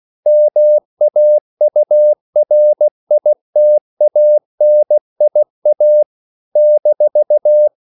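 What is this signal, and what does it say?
Morse code, heard as a keyed tone, about 16 wpm, 600 Hz. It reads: MAURITANIA -
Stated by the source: CW practice of national capitals at 16 words per minute, from ON6ZQ.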